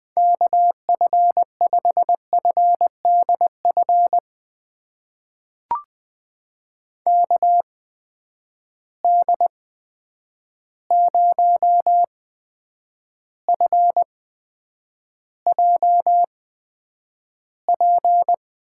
Morse code, heard as a keyed tone, about 20 wpm, spaced as for 9 wpm, 700 Hz. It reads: KF5FDF KD0FJP
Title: USA Call Signs - 6 Characters - 8x Character Spacing - 20wpm